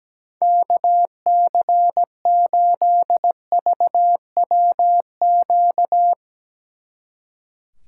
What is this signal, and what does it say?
Morse code, heard as a keyed tone, about 18 wpm, 700 Hz. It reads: KC8VWQ
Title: Speed Racing - USA 2x3 Call Signs 17wpm